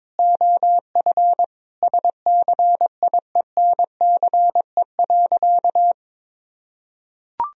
Morse code, heard as a keyed tone, about 22 wpm, 700 Hz. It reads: OF SCIENCE.
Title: Sentences from Top 1000 Words - Review Part 2 - 22wpm